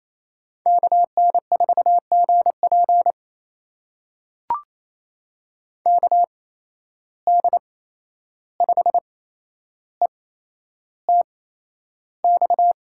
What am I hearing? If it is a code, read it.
Morse code, heard as a keyed tone, about 28 wpm, spaced as for 11 wpm, 700 Hz. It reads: KN4GP KD5ETX
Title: USA Call Signs - 8x Character Spacing - 28wpm